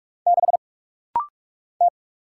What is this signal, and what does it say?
Morse code, heard as a keyed tone, about 44 wpm, 700 Hz. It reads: 6 T
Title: Single Letter-Number - Rapid-Fire 45wpm